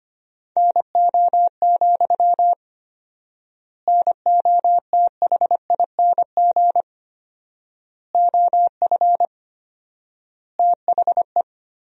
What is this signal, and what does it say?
Morse code, heard as a keyed tone, about 25 wpm, 700 Hz. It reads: NO, NOTHING OF THE
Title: 4x Word-Spacing - Sentences from Top 500 Words - 25wpm